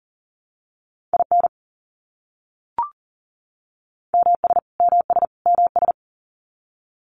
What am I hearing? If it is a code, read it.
Morse code, heard as a keyed tone, about 40 wpm, 700 Hz. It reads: IN MS MS MS
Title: US States Repeated 3x 40wpm